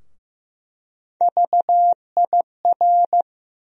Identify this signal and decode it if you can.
Morse code, heard as a keyed tone, about 15 wpm, 700 Hz. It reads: VIR
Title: USA Call Sign Postfixes - 3 Letters 15wpm